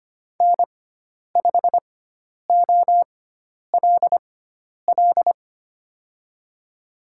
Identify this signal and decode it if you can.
Morse code, heard as a keyed tone, about 25 wpm, spaced as for 10 wpm, 700 Hz. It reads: N5OLL
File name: Lesson 16g - Letter c - Call Signs 25wpm